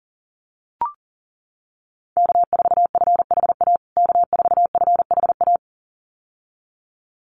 Morse code, heard as a keyed tone, about 40 wpm, 700 Hz. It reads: K4FHA K4FHA
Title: USA Call Signs Repeated 2x 40wpm